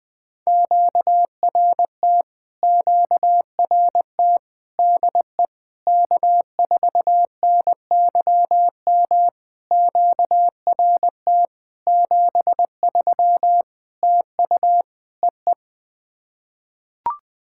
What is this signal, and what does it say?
Morse code, heard as a keyed tone, about 20 wpm, 700 Hz. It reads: QRT QRT DE K4NYM QRT 73 TU EE
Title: POTA QSO Practice - 20wpm